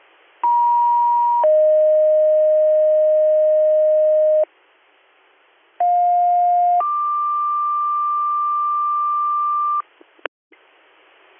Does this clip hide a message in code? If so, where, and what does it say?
Yes: two-tone page from 0.4 to 9.8 s, heard as a one-second tone, then a three-second tone, about 950 then 620 Hz, about 710 then 1180 Hz.